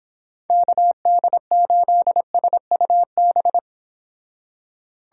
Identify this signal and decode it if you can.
Morse code, heard as a keyed tone, about 26 wpm, 700 Hz. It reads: KD8SUB